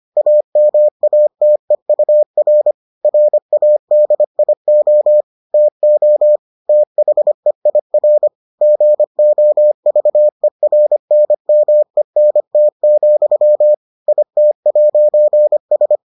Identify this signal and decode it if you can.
Morse code, heard as a keyed tone, about 25 wpm, 600 Hz. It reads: AMATEUR RADIO TO THEIR GOVERNMENT, IT'S